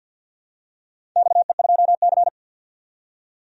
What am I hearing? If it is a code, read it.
Morse code, heard as a keyed tone, about 49 wpm, 700 Hz. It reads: XE2C
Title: DXCC-100 Call Signs - 4 Characters - 50wpm